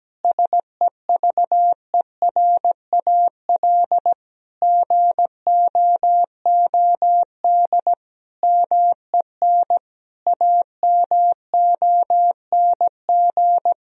Morse code, heard as a keyed tone, about 17 wpm, 700 Hz. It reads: SEVERAL GOOD MEN AMONG
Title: Sentences from Top 1000 Words - Review Part 2 - 17wpm